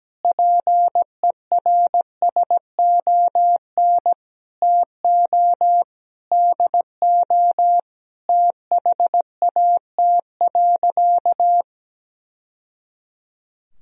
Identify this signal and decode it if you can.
Morse code, heard as a keyed tone, about 17 wpm, 700 Hz. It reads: PERSON TO DO THAT.
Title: Sentences from Top 1000 Words - Review Part 3 - 17wpm